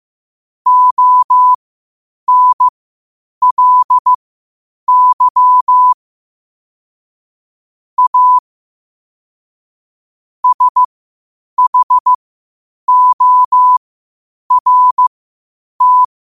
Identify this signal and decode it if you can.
Morse code, heard as a keyed tone, about 15 wpm, spaced as for 8 wpm, 1000 Hz. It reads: ONLY A SHORT
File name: qotdfiles / 8wpm